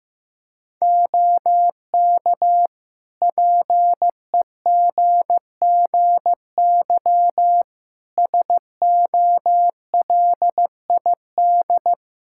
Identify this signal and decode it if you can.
Morse code, heard as a keyed tone, about 15 wpm, 700 Hz. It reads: OK PEGGY SOLID
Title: Practice QSO - Exchange 3 - More Info - 15wpm